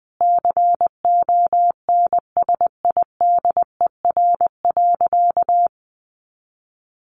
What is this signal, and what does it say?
Morse code, heard as a keyed tone, about 20 wpm, 700 Hz. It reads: CONSIDER.
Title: Sentences from Top 900 Words 20wpm